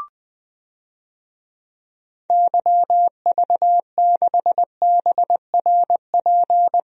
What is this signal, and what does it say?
Morse code, heard as a keyed tone, about 20 wpm, 700 Hz. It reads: YV6BRP
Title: DXCC-100 Call Signs - 6 Characters - Repeated 2x - 20wpm